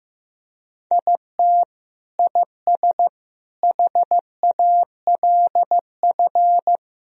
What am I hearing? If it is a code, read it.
Morse code, heard as a keyed tone, about 15 wpm, 700 Hz. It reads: IT IS HALF
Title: Sentences from Top 1000 Words - Review Part 1 - 15wpm